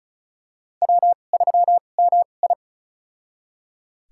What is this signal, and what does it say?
Morse code, heard as a keyed tone, about 36 wpm, spaced as for 25 wpm, 700 Hz. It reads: W3MI
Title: USA Call Signs - 2x Character Spacing - 35wpm